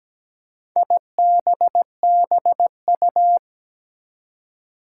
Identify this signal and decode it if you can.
Morse code, heard as a keyed tone, about 17 wpm, 700 Hz. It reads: IBBU
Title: Four Challenging Letter-Number Combinations 17wpm